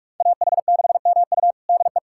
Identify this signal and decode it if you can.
Morse code, heard as a keyed tone, about 46 wpm, 700 Hz. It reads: AH6MU DE
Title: USA Callsign DE Callsign - 5 Characters - 45wpm